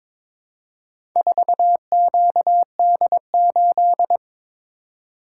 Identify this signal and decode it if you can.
Morse code, heard as a keyed tone, about 22 wpm, 700 Hz. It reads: 4QD8
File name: practice / 22wpm / Four Letter-Number Combinations - 22wpm